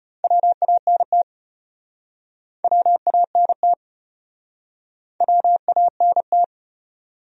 Morse code, heard as a keyed tone, about 33 wpm, 700 Hz. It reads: WANT WANT WANT